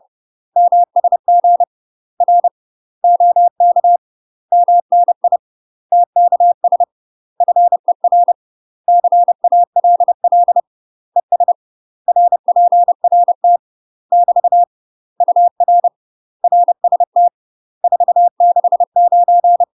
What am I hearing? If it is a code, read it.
Morse code, heard as a keyed tone, about 30 wpm, 700 Hz. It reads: MSG R OK MNI TKS FER CALL ES RPRT <BT> UR RST 469
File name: UK MASTER QSOs TRAINING / 30WPM / UK MASTER QSOs 0010